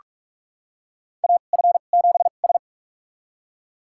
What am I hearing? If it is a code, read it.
Morse code, heard as a keyed tone, about 46 wpm, spaced as for 32 wpm, 700 Hz. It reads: AF7S
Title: USA Call Signs - 4 Characters - 2x Character Spacing - 45wpm